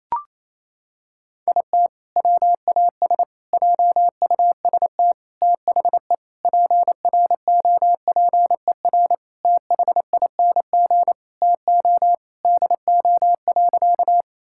Morse code, heard as a keyed tone, about 28 wpm, 700 Hz. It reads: IT WAS JUST THE PROPER THING TO DO.
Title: Sentences from Top 1000 Words - Review Part 5 - 28wpm